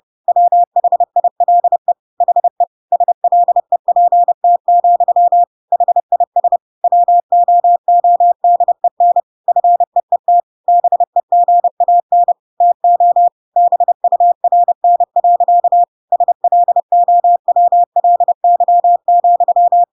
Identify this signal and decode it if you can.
Morse code, heard as a keyed tone, about 30 wpm, 700 Hz. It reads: WHILE HE SLEPT, HIS WOODEN FEET BEGAN TO BURN. SLOWLY,
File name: Pinocchio / Pinocchio ch6 30 30 ik1hge